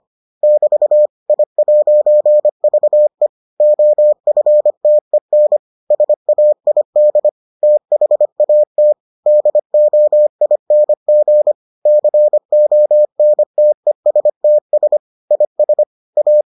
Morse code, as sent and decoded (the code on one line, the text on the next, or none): -...- / .. .----. ...- . / --- ..-. - . -. / ... .- .. -.. / - .... .- - / -.. --- .. -. --. / -.-. --- -. - . ... - ... / .. ... / .-
<BT> I'VE OFTEN SAID THAT DOING CONTESTS IS A